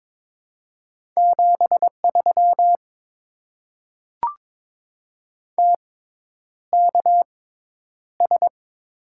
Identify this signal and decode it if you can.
Morse code, heard as a keyed tone, about 22 wpm, spaced as for 10 wpm, 700 Hz. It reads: 73 TKS
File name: Top 90 QSO Elements - 6x Character Spacing - 22wpm